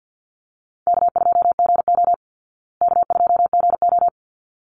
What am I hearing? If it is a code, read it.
Morse code, heard as a keyed tone, about 50 wpm, 700 Hz. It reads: K2GO K2GO
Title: Easy Call Signs Repeated 2x - 50wpm